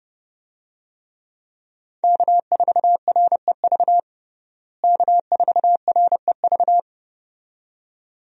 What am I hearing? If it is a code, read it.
Morse code, heard as a keyed tone, about 30 wpm, 700 Hz. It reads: K4REV K4REV
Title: USA Call Signs - 5 Characters - Repeated 2x - 30wpm